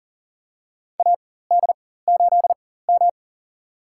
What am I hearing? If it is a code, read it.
Morse code, heard as a keyed tone, about 40 wpm, spaced as for 19 wpm, 700 Hz. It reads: AD8M
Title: USA Call Signs - 4 Characters - 4x Character Spacing - 40wpm